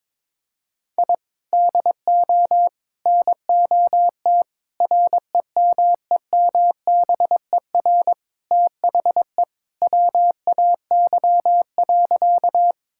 Morse code, heard as a keyed tone, about 21 wpm, 700 Hz. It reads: I DO NOT REMEMBER THE WAY.
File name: practice / 22wpm / Sentences with 4-Word Phrases - 22wpm